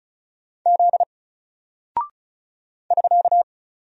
Morse code, heard as a keyed tone, about 36 wpm, 700 Hz. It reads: Z <SK>
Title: Single Character (Letter:Number:Punctuation:Prosign) 35wpm